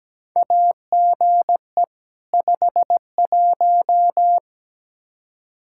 Morse code, heard as a keyed tone, about 17 wpm, 700 Hz. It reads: AGE 51
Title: Practice QSO - Exchange 3 - More Info - 17wpm